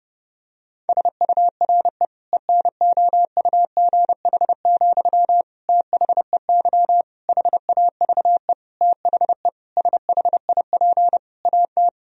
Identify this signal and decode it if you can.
Morse code, heard as a keyed tone, about 32 wpm, 700 Hz. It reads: SURE ENOUGH, THEY HAVE THE SHIP AT
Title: Sentences from Top 1000 Words - Review Part 2 - 30wpm